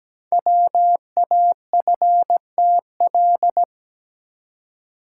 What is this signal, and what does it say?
Morse code, heard as a keyed tone, about 17 wpm, 700 Hz. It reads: WAFTL
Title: Code Groups - 5 Letters - 17wpm